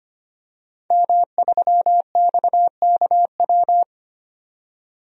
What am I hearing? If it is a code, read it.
Morse code, heard as a keyed tone, about 25 wpm, 700 Hz. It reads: M3XKW
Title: DXCC-100 Call Signs - 5 Characters - 25wpm